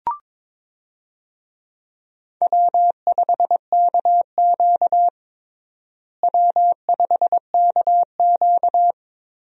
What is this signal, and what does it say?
Morse code, heard as a keyed tone, about 21 wpm, 700 Hz. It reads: W5KQ W5KQ
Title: USA Call Signs - 4 Characters - Repeated 3x - 22wpm